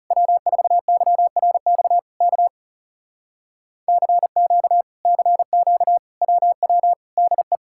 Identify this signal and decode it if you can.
Morse code, heard as a keyed tone, about 40 wpm, 700 Hz. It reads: W4YRX K CQ CQ WW DE